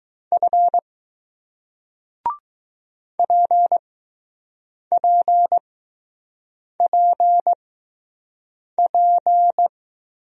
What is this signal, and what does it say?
Morse code, heard as a keyed tone, about 20 wpm, spaced as for 6 wpm, 700 Hz. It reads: F PPPP